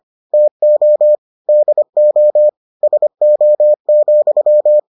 Morse code, heard as a keyed tone, about 25 wpm, 600 Hz. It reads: TO DO SO,